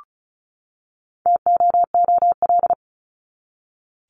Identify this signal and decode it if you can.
Morse code, heard as a keyed tone, about 34 wpm, 700 Hz. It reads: TOOL
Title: Top 751-1000 Words 35wpm